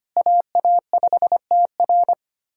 Morse code, heard as a keyed tone, about 25 wpm, 690 Hz. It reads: AA5TR
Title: SOTA QSO Practice 25wpm